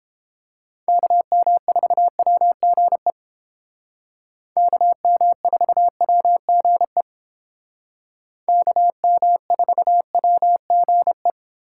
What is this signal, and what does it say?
Morse code, heard as a keyed tone, about 30 wpm, 700 Hz. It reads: KM4WGE KM4WGE KM4WGE